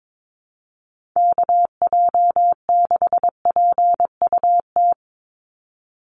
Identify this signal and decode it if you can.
Morse code, heard as a keyed tone, about 22 wpm, 700 Hz. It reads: KJ6PUT
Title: USA Call Signs 22wpm